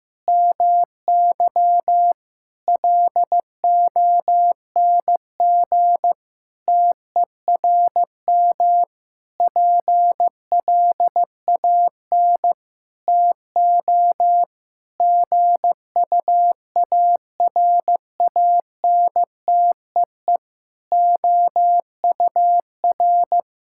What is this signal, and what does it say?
Morse code, heard as a keyed tone, about 15 wpm, 700 Hz. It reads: MY LONG TERM PLAN TO GUARANTEE OUR
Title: Sentences from Top 1600 Words 15wpm